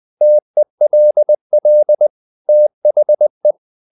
Morse code, heard as a keyed tone, about 20 wpm, 600 Hz. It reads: TELL THE
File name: the old man and the sea part 2 20wpm